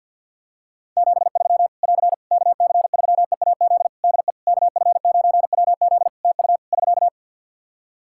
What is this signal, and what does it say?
Morse code, heard as a keyed tone, about 51 wpm, 700 Hz. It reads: 73 <AR> KC3EAZ DE KF9WZ TU <SK>
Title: Practice QSO - Exchange 4 - Ending - 50wpm